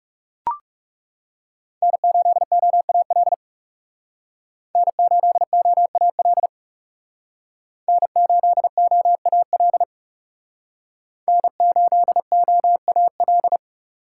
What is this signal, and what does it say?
Morse code, heard as a keyed tone, about 39 wpm, 700 Hz. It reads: N8OAL N8OAL N8OAL N8OAL